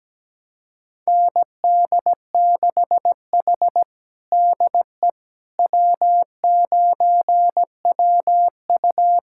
Morse code, heard as a keyed tone, about 17 wpm, 700 Hz. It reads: ND6H DE W9WU
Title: USA Callsign DE Callsign - 4 Characters - 17wpm